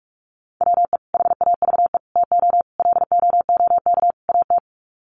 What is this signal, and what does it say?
Morse code, heard as a keyed tone, about 44 wpm, 700 Hz. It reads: WE HAVE TO LOOK AT
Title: Sets of 5 Words Encore2 45wpm